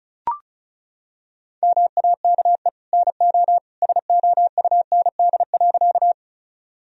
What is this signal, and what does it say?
Morse code, heard as a keyed tone, about 36 wpm, 700 Hz. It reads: MAKE NO SOUND.
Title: Sentences from Top 1000 Words - Review Part 1 - 35wpm